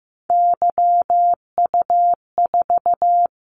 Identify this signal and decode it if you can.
Morse code, heard as a keyed tone, about 15 wpm, 700 Hz. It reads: YU4
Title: Callsign Prefixes for Top 25 DXCC - 15wpm